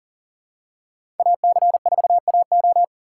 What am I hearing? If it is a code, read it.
Morse code, heard as a keyed tone, about 40 wpm, 700 Hz. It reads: AC4AO
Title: USA Call Signs - 5 Characters - 4x Character Spacing - 40wpm